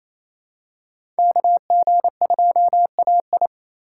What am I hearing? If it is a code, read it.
Morse code, heard as a keyed tone, about 28 wpm, 700 Hz. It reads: KG2AI
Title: USA Call Signs - 5 Characters - 2x Character Spacing - 28wpm